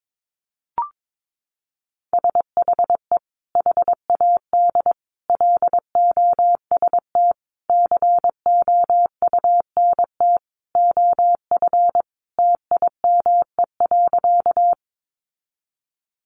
Morse code, heard as a keyed tone, about 22 wpm, 700 Hz. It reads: SHE HAD LOST COUNT OF TIME.
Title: Sentences from Top 1500 Words 22wpm